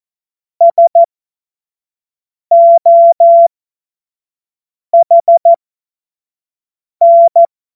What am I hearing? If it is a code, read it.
Morse code, heard as a keyed tone, about 14 wpm, spaced as for 5 wpm, 680 Hz. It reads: SOHN
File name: Morsekurs / Witze / Witz 14. März 2023 25 F70